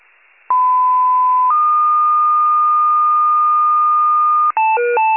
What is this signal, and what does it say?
Two-tone page, heard as a one-second tone, then a three-second tone, about 1010 then 1220 Hz. It is heard from 0.5 to 4.5 s.